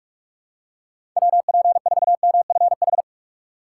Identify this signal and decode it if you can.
Morse code, heard as a keyed tone, about 46 wpm, 700 Hz. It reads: WP4MFH
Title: USA Call Signs - 6 Characters - 2x Character Spacing - 45wpm